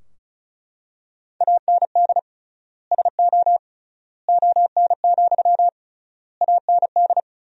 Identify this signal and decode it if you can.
Morse code, heard as a keyed tone, about 36 wpm, 700 Hz. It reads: AND SO ON, AND
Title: 3x Word-Spacing - Sentences from Top 100 Words - 35wpm